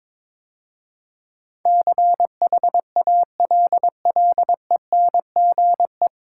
Morse code, heard as a keyed tone, about 22 wpm, 700 Hz. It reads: CHALLENGE
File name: 9 Letter Words - First 3 Letters Given - 22wpm